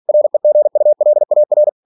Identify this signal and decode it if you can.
Morse code, heard as a keyed tone, about 46 wpm, 600 Hz. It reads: REGULAR
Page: Never 47wpm